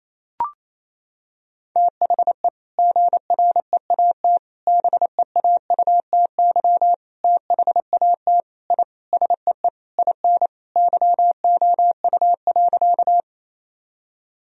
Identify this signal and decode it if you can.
Morse code, heard as a keyed tone, about 28 wpm, 700 Hz. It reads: THE GREAT BEAUTY THAT I SEE IN YOU.
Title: Sentences from Top 1000 Words - Review Part 2 - 28wpm